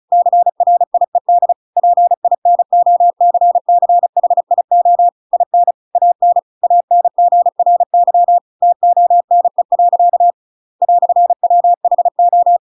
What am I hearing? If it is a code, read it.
Morse code, heard as a keyed tone, about 35 wpm, 700 Hz. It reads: CRIED PINOCCHIO IN AN ANGRY TONE. "WHO
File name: Pinocchio / Pinocchio ch7 35 35 ik1hge